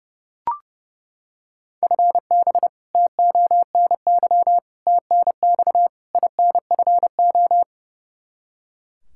Morse code, heard as a keyed tone, about 30 wpm, 700 Hz. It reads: FB TONY TNX INFO
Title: Practice QSO - Exchange 3 - More Info - 30wpm